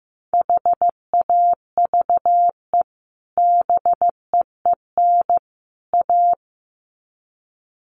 Morse code, heard as a keyed tone, about 15 wpm, 700 Hz. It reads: HAVE BEEN A
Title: Sets of 3 Words 15wpm